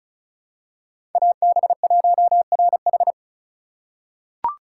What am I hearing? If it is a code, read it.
Morse code, heard as a keyed tone, about 36 wpm, 700 Hz. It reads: AB1RH
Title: USA Call Signs - 5 Characters - 2x Character Spacing - 35wpm